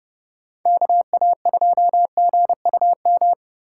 Morse code, heard as a keyed tone, about 35 wpm, 700 Hz. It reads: KA2GUM
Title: Speed Racing - USA 2x3 Call Signs 30wpm